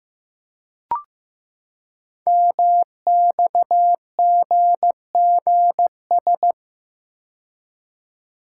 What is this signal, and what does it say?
Morse code, heard as a keyed tone, about 15 wpm, 700 Hz. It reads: MXGGS
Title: Code Groups - 5 Letters - 15wpm